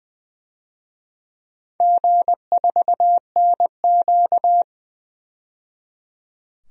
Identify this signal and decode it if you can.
Morse code, heard as a keyed tone, about 20 wpm, 700 Hz. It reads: G4NQ
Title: DXCC-100 Call Signs - 4 Characters - 20wpm